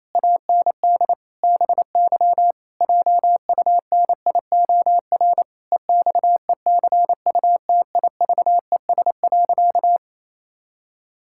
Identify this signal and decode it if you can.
Morse code, heard as a keyed tone, about 28 wpm, 700 Hz. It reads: AND BY JUNIOR EXECUTIVES.
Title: Pangram Challenge v2 - 28wpm